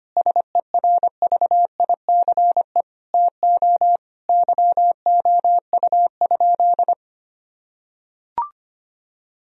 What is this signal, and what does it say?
Morse code, heard as a keyed tone, about 25 wpm, 700 Hz. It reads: SERVICE TO YOU?
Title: Sentences from Top 1000 Words - Review Part 4 - 25wpm